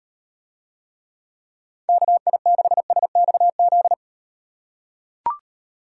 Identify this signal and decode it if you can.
Morse code, heard as a keyed tone, about 38 wpm, 700 Hz. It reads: KI6SXZ